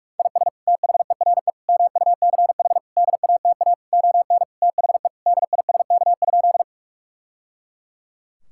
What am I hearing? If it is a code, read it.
Morse code, heard as a keyed tone, about 41 wpm, 700 Hz. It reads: IS THERE MUCH DATA ON THE DISK?
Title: Sentences from Top 1000 Words - Review Part 6 - 45wpm